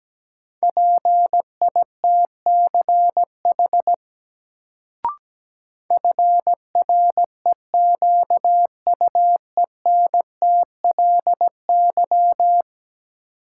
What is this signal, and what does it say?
Morse code, heard as a keyed tone, about 17 wpm, 700 Hz. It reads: PITCH FREQUENTLY
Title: Sentences from Top 1300 Words - New Word Drill Alt - 17wpm